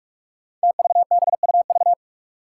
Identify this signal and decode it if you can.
Morse code, heard as a keyed tone, about 44 wpm, 700 Hz. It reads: TVBUV
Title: Five Challenging Letter-Number Combinations 45wpm